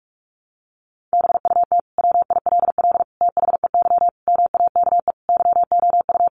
Morse code, heard as a keyed tone, about 44 wpm, 700 Hz. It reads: BUT WILL THEY MAKE YOU
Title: Sentences from Top 1000 Words 45wpm